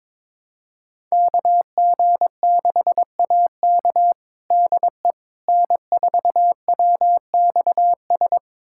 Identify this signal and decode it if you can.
Morse code, heard as a keyed tone, about 22 wpm, 700 Hz. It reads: KG6AK DE N4WXS
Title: USA Callsign DE Callsign - 5 Characters - 22wpm